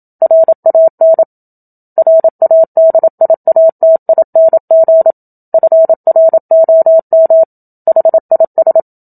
Morse code, heard as a keyed tone, about 27 wpm, 650 Hz. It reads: RAN RADIATING FROM HIS
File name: Book Two / 01-Chapter 1 - 27 WPM